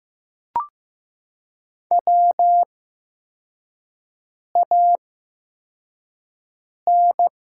Morse code, heard as a keyed tone, about 15 wpm, spaced as for 4 wpm, 700 Hz. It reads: WAN
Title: 8x Character Spacing - Top 500 Words - 15wpm